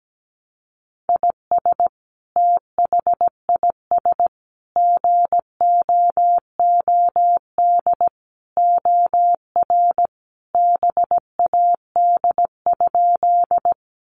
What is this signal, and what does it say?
Morse code, heard as a keyed tone, about 17 wpm, 700 Hz. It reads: IS THIS GOOD OR BAD?
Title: Sentences from Top 700 Words 17wpm